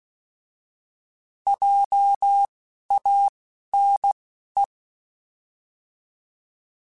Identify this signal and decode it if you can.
Morse code, heard as a keyed tone, about 16 wpm, spaced as for 11 wpm, 800 Hz. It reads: JANE
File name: Alphabetical / Alphabetical mp3s / Js @10